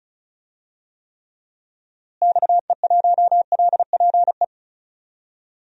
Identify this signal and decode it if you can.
Morse code, heard as a keyed tone, about 36 wpm, 700 Hz. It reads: XE1LPE
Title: DXCC-100 Call Signs - 6 Characters - Repeated 3x - 35wpm